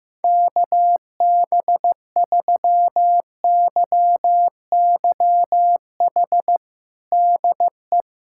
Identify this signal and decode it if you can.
Morse code, heard as a keyed tone, about 16 wpm, 700 Hz. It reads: KB3YYH DE
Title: USA Callsign DE Callsign - 6 Characters - 15wpm